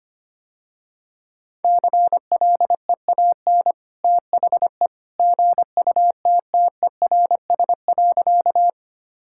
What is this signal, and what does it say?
Morse code, heard as a keyed tone, about 25 wpm, 700 Hz. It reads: CLEAN THE GUTTERS.
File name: Easy Sentences with Hints - 25wpm